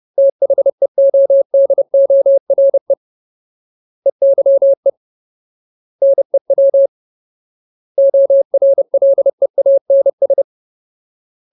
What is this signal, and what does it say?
Morse code, heard as a keyed tone, about 30 wpm, 550 Hz. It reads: THEODORE EYE NEW ORLEANS
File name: Ultimate Word List / 30WPM Ultimate